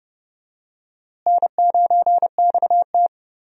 Morse code, heard as a keyed tone, about 30 wpm, 700 Hz. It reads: N9XT
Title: USA Call Signs - 4 Characters - Repeated 3x - 30wpm